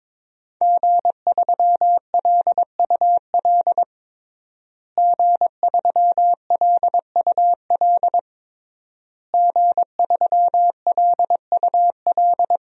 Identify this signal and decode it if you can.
Morse code, heard as a keyed tone, about 21 wpm, 700 Hz. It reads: G3LUL G3LUL G3LUL
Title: DXCC-100 Call Signs - 5 Characters - Repeated 3x - 22wpm